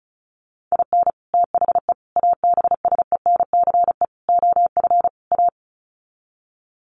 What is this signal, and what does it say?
Morse code, heard as a keyed tone, about 34 wpm, 700 Hz. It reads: IN THE ABSENCE OF A
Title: Sets of 5 Words Encore 35wpm